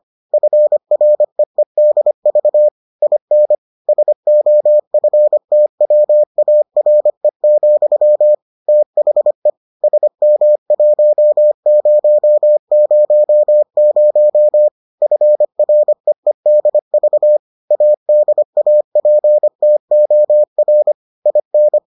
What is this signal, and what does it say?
Morse code, heard as a keyed tone, about 25 wpm, 600 Hz. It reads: FREEDV IN SOFTWARE, THE SM1000 FREEDV ADAPTOR IN